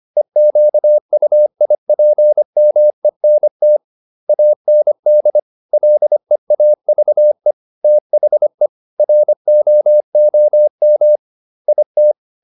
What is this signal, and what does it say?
Morse code, heard as a keyed tone, about 25 wpm, 600 Hz. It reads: EQUIPMENT AND LEAVE THE ROOM IT